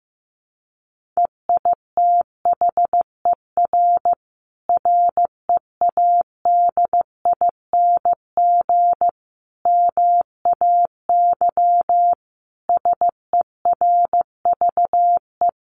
Morse code, heard as a keyed tone, about 15 wpm, 700 Hz. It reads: EITHER READING MAY SERVE
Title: Sentences from Top 1400 Words 15wpm